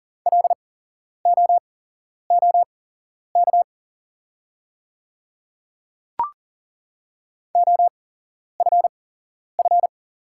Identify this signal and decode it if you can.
Morse code, heard as a keyed tone, about 43 wpm, spaced as for 11 wpm, 700 Hz. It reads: LOOK OFF
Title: 8x Character Spacing - Top 200 Words - 40wpm